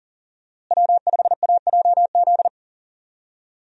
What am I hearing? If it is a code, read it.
Morse code, heard as a keyed tone, about 40 wpm, 700 Hz. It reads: W5AJZ
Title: USA Call Signs - 5 Characters - Repeated 3x - 40wpm